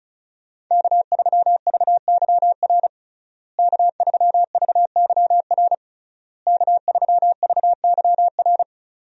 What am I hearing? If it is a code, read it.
Morse code, heard as a keyed tone, about 34 wpm, 700 Hz. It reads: K3VYR K3VYR K3VYR
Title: USA Call Signs - 5 Characters - Repeated 3x - 35wpm